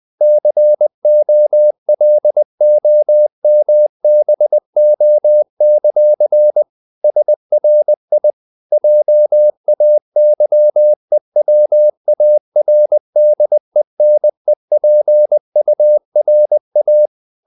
Morse code, heard as a keyed tone, about 20 wpm, 600 Hz. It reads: COLOMBO; SRI JAYEWARDENEPURA
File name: CW / Practice / NationalCapitals20